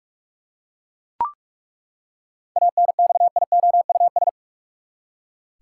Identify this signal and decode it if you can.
Morse code, heard as a keyed tone, about 41 wpm, 700 Hz. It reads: ANXIOUS
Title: Seven Letter Words - 45wpm